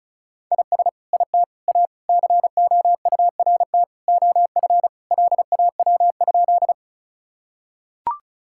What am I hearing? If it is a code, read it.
Morse code, heard as a keyed tone, about 36 wpm, 700 Hz. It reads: IS IT A COURT OF LAW?
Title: Sentences from Top 1600 Words 35wpm